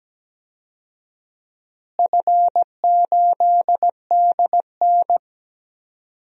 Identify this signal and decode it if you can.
Morse code, heard as a keyed tone, about 17 wpm, 700 Hz. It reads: F8DN
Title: DXCC-100 Call Signs - 4 Characters - Repeated 2x - 17wpm